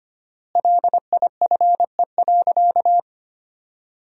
Morse code, heard as a keyed tone, about 25 wpm, 700 Hz. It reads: LIFE.
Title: 4x Word-Spacing - Sentences from Top 400 Words - 25wpm